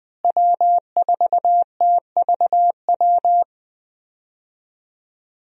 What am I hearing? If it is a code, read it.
Morse code, heard as a keyed tone, about 20 wpm, 700 Hz. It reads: W4TVW
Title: USA Call Signs - 5 Characters - Repeated 3x - 20wpm